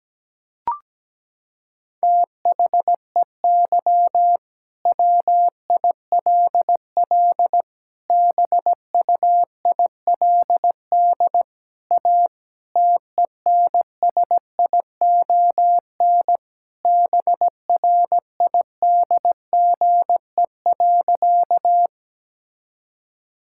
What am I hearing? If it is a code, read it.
Morse code, heard as a keyed tone, about 17 wpm, 700 Hz. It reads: THEY WILL BUILD A TENSION BRIDGE.
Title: Sentences from Top 1600 Words 17wpm